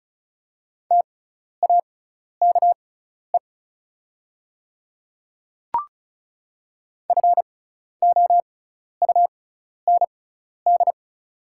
Morse code, heard as a keyed tone, about 36 wpm, spaced as for 12 wpm, 690 Hz. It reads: TAKE FOUND
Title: 6x Character Spacing - Top 300 Words - 35wpm